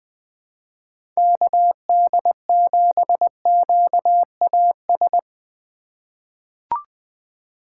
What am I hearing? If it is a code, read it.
Morse code, heard as a keyed tone, about 20 wpm, 700 Hz. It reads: KD7QAS
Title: USA Call Signs - 6 Characters - 4x Character Spacing - 20wpm